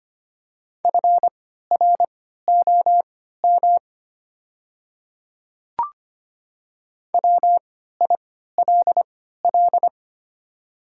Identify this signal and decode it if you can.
Morse code, heard as a keyed tone, about 25 wpm, spaced as for 14 wpm, 690 Hz. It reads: FROM WILL
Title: 3x Character Spacing - Top 100 Words - 25wpm